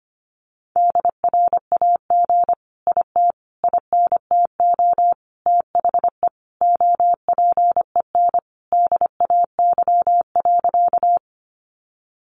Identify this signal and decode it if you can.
Morse code, heard as a keyed tone, about 25 wpm, 700 Hz. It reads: DRAG IT INTO THE OPEN DAY.
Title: Sentences from Top 1500 Words 25wpm